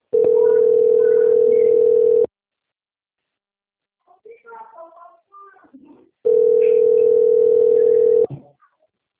Ringback tone (North America), 440 + 480 Hz.
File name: out- - - - - .49131